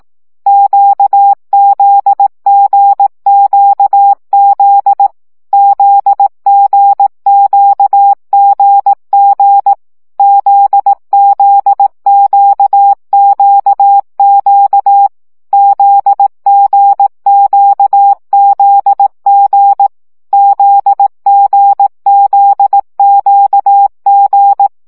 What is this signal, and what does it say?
Morse code, heard as a keyed tone, about 18 wpm, 800 Hz. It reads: QZGQZ ZGQGG ZZQQQ ZGQZG ZGZQG
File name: CW gyakorlas g z q 15 baud